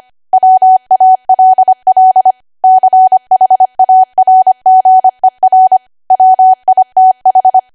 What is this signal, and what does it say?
Morse code, heard as a keyed tone, about 25 wpm, 750 Hz. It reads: WALL CHARGER WITH
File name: cwa / W1AW-Level3 / 25WPM